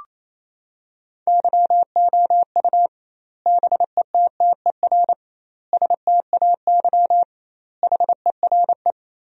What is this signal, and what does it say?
Morse code, heard as a keyed tone, about 28 wpm, 700 Hz. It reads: YOU BETTER STAY HERE
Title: 2x Word-Spacing - Sentences from Top 500 Words - 28wpm